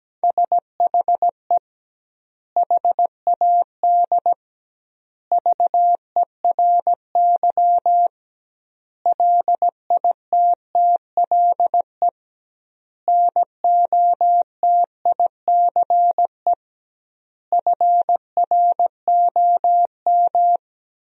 Morse code, heard as a keyed tone, about 17 wpm, 700 Hz. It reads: SHE HAD VERY LITTLE NOTICE FROM
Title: 2x Word-Spacing - Sentences from Top 400 Words - 17wpm